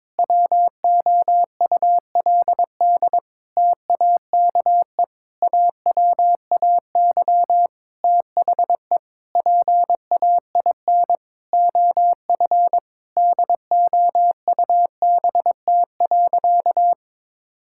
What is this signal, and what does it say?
Morse code, heard as a keyed tone, about 22 wpm, 700 Hz. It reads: WOULD TAKE AWAY THE PAIN OF DOUBT.
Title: Sentences from Top 1600 Words 22wpm